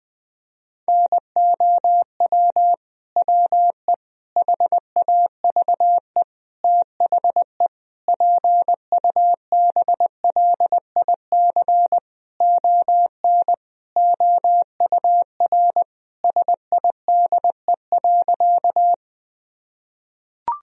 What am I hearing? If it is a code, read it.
Morse code, heard as a keyed tone, about 20 wpm, 700 Hz. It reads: NOW WE HAVE THE PUBLIC ON OUR SIDE.
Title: Sentences from Top 1000 Words - Review Part 3 - 20wpm